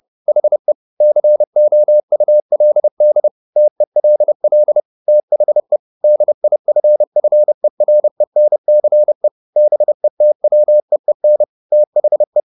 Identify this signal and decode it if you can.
Morse code, heard as a keyed tone, about 30 wpm, 600 Hz. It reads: HE COULD TELL THE DIFFERENCE BETWEEN THE
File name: the old man and the sea / the old man and the sea 1 30wpm